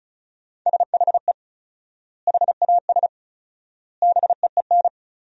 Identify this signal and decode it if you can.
Morse code, heard as a keyed tone, about 36 wpm, 700 Hz. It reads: SHE HAS BEEN
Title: 4x Word-Spacing - Sentences from Top 200 Words - 35wpm